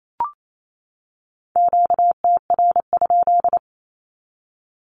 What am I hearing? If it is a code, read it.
Morse code, heard as a keyed tone, about 28 wpm, 700 Hz. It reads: QTR?
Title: Q-Codes 28wpm